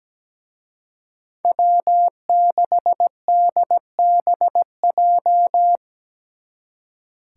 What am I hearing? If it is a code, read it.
Morse code, heard as a keyed tone, about 20 wpm, 700 Hz. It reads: W6DBJ
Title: Speed Racing - USA 1x3 Call Signs 17wpm